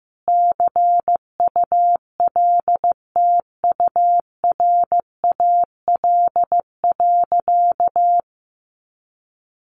Morse code, heard as a keyed tone, about 15 wpm, 700 Hz. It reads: CULTURAL.
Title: Sentences from Top 1500 Words 15wpm